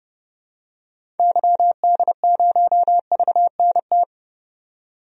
Morse code, heard as a keyed tone, about 30 wpm, 700 Hz. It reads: YD0VNT
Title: DXCC-100 Call Signs - 6 Characters - 30wpm